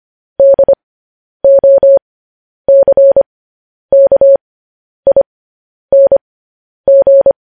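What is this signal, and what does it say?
Morse code, heard as a keyed tone, about 25 wpm, spaced as for 10 wpm, 550 Hz. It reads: DOCKING
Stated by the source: Morse code news headlines practice, at 10 WPM.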